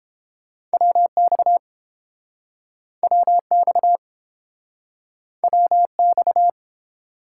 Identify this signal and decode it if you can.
Morse code, heard as a keyed tone, about 32 wpm, 700 Hz. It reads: WX WX WX